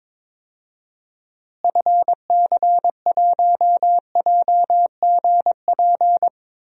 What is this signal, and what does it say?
Morse code, heard as a keyed tone, about 22 wpm, 700 Hz. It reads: FC1JGP
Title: DXCC-100 Call Signs - 6 Characters - Repeated 3x - 22wpm